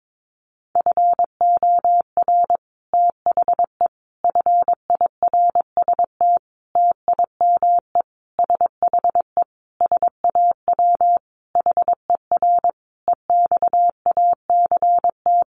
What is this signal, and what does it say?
Morse code, heard as a keyed tone, about 22 wpm, 700 Hz. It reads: FOR THE FIRST TIME SHE SAW HER EXACT
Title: Sentences from Top 1300 Words 22wpm